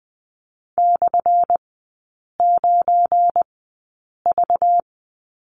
Morse code, heard as a keyed tone, about 20 wpm, spaced as for 8 wpm, 700 Hz. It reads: /9V
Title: 198 Lesson 32e - Letter k - Three Characters 20wpm